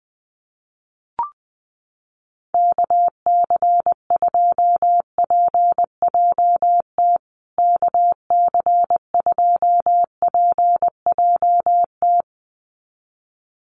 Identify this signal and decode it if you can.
Morse code, heard as a keyed tone, about 20 wpm, 700 Hz. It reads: KC2PJT KC2PJT
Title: USA Call Signs Repeated 2x 20wpm